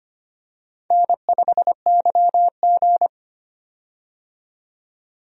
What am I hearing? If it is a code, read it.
Morse code, heard as a keyed tone, about 25 wpm, 700 Hz. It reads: N5YG
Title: USA Call Signs - 4 Characters - 25wpm